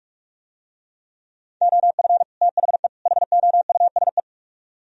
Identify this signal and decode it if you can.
Morse code, heard as a keyed tone, about 44 wpm, 700 Hz. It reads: OF THE HOUSE